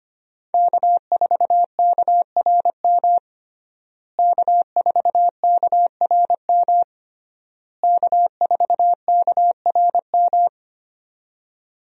Morse code, heard as a keyed tone, about 25 wpm, 700 Hz. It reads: K4KRM K4KRM K4KRM
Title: USA Call Signs - 5 Characters - Repeated 3x - 25wpm